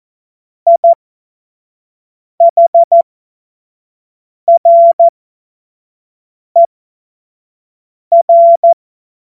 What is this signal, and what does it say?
Morse code, heard as a keyed tone, about 14 wpm, spaced as for 5 wpm, 680 Hz. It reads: IHRER
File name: Morsekurs / Witze / Witz 26. April 2022 25 F70